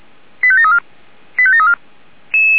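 Five-tone selective call (ZVEI), heard as quick runs of five tones, tones of about 70 ms. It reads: 85734 85734
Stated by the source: ZVEI five-tone selective call received on an SDR.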